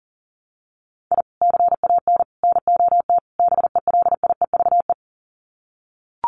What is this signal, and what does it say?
Morse code, heard as a keyed tone, about 40 wpm, 700 Hz. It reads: I CAN NOT BELIEVE
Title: Sets of 4 Words 40wpm